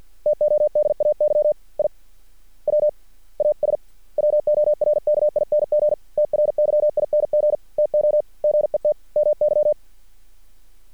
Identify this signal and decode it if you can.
Morse code, heard as a keyed tone, about 50 wpm, spaced as for 43 wpm, 600 Hz. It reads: TODAY I WAS WORKING TRYING TO GET MY
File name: archived / CRSnet